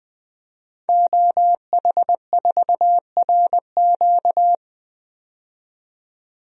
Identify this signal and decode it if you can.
Morse code, heard as a keyed tone, about 21 wpm, 700 Hz. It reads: OH4RQ
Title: DXCC-100 Call Signs - 5 Characters - 20wpm